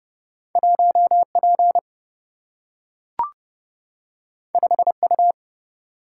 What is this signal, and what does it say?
Morse code, heard as a keyed tone, about 30 wpm, 700 Hz. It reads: 1P 5U
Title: Single-Letter Single-Number 30wpm